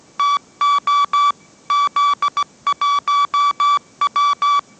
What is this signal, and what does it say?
Morse code, heard as a keyed tone, about 17 wpm, 1150 Hz. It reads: TO Z1W
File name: mrx / mrx080 18wpm